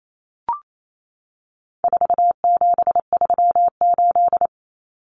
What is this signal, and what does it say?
Morse code, heard as a keyed tone, about 28 wpm, 700 Hz. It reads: 4738
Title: Numbers - 4-Digit 28wpm